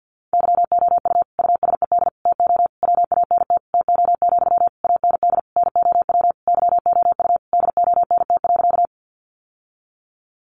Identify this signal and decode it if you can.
Morse code, heard as a keyed tone, about 50 wpm, 700 Hz. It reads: YOU USED TO WANT TO, AND NOW YOU DONT.
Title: Sentences from Top 600 Words 50wpm